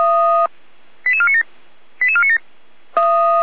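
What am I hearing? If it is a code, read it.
Five-tone selective call (ZVEI), heard as quick runs of five tones, tones of about 70 ms. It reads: 88497 88497
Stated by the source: ZVEI five-tone selective call received on an SDR.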